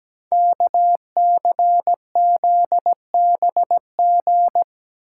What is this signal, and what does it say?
Morse code, heard as a keyed tone, about 17 wpm, 700 Hz. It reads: KCZBG
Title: Code Groups - 5 Letters - 17wpm